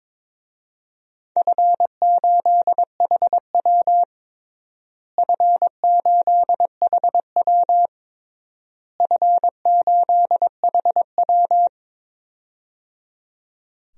Morse code, heard as a keyed tone, about 22 wpm, 700 Hz. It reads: F8HW F8HW F8HW
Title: DXCC-100 Call Signs - 4 Characters - Repeated 3x - 22wpm